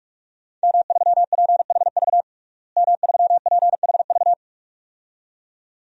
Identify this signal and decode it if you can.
Morse code, heard as a keyed tone, about 46 wpm, 700 Hz. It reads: M3PHV M3PHV
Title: DXCC-100 Call Signs - 5 Characters - Repeated 2x - 45wpm